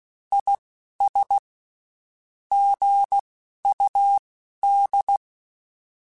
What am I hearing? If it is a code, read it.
Morse code, heard as a keyed tone, about 16 wpm, spaced as for 11 wpm, 800 Hz. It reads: IS GUD
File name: (spaced) / QSOs Spaced-mp3s / WA7QOX 1 @15 sp